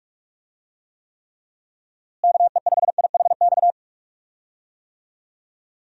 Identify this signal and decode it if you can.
Morse code, heard as a keyed tone, about 42 wpm, 700 Hz. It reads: KE5IHX
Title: USA Call Signs - 6 Characters - Repeated 3x - 45wpm